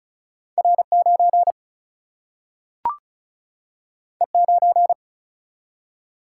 Morse code, heard as a keyed tone, about 36 wpm, 700 Hz. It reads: R9 E9
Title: Single-Letter Single-Number 35wpm